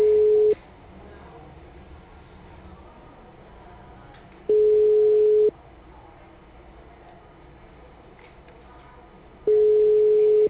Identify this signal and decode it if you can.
Ringback tone (Europe, 425 Hz).
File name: out- -2014- - - .6